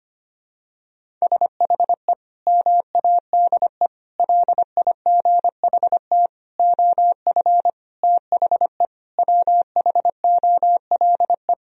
Morse code, heard as a keyed tone, about 25 wpm, 700 Hz. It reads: SHE MADE LIGHT OF THE WHOLE
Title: Sentences from Top 1000 Words - Review Part 1 - 25wpm